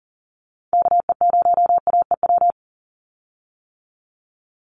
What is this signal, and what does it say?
Morse code, heard as a keyed tone, about 40 wpm, 700 Hz. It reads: KE0AEW
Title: USA Call Signs 40wpm